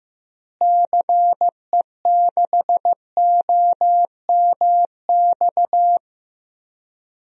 Morse code, heard as a keyed tone, about 15 wpm, 700 Hz. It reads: CE6OMX
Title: DXCC-100 Call Signs - 6 Characters - Repeated 2x - 15wpm